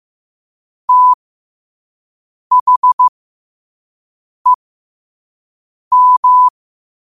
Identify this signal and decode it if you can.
Morse code, heard as a keyed tone, about 15 wpm, spaced as for 5 wpm, 1000 Hz. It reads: THEM
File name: qotdfiles / 5wpm